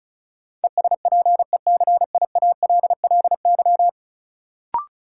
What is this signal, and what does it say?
Morse code, heard as a keyed tone, about 34 wpm, 700 Hz. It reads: ESPECIALLY
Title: Sentences from Top 700 Words - New Word Drill - 35wpm